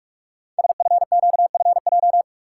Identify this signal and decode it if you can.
Morse code, heard as a keyed tone, about 44 wpm, 700 Hz. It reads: SFQFJ
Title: Code Groups - 5 Letters - 45wpm